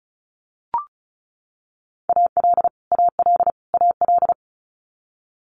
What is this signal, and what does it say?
Morse code, heard as a keyed tone, about 34 wpm, 700 Hz. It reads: AL AL AL
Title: Multi-letter Phonemes Repeated 3x 35wpm